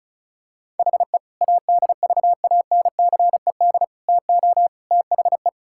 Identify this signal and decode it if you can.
Morse code, heard as a keyed tone, about 36 wpm, 700 Hz. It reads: HE ADVANCED TO THE
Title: Sentences from Top 1600 Words 35wpm